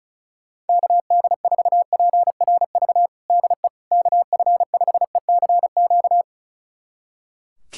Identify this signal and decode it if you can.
Morse code, heard as a keyed tone, about 36 wpm, 700 Hz. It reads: KD4PRV DE KF5ECQ
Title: USA Callsign DE Callsign - 6 Characters - 35wpm